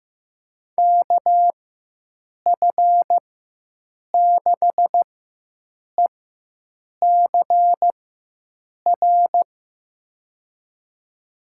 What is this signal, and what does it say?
Morse code, heard as a keyed tone, about 15 wpm, spaced as for 7 wpm, 700 Hz. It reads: KF6ECR
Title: USA Call Signs - 6 Characters - 4x Character Spacing - 15wpm